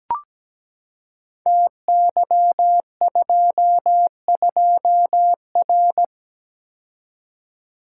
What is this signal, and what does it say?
Morse code, heard as a keyed tone, about 17 wpm, 700 Hz. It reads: TY22R
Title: Code Groups - 5 Letter:Numbers - 17wpm